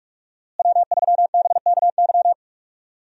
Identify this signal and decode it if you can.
Morse code, heard as a keyed tone, about 45 wpm, 700 Hz. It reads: W3BKY
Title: USA Call Signs - 5 Characters - Repeated 2x - 45wpm